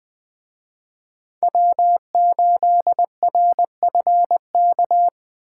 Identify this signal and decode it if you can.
Morse code, heard as a keyed tone, about 19 wpm, 700 Hz. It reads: W8RFK